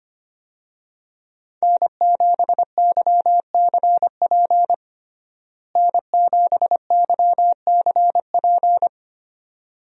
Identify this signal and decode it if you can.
Morse code, heard as a keyed tone, about 25 wpm, 700 Hz. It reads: N7YCP N7YCP
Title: USA Call Signs - 5 Characters - Repeated 2x - 25wpm